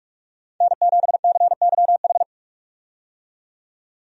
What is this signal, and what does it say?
Morse code, heard as a keyed tone, about 46 wpm, 700 Hz. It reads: N7CYH
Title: USA Call Signs - 5 Characters - 45wpm